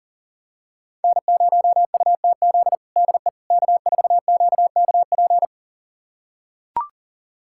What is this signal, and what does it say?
Morse code, heard as a keyed tone, about 40 wpm, 700 Hz. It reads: N0UTZ DE K4QKP
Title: USA Callsign DE Callsign - 5 Characters - 40wpm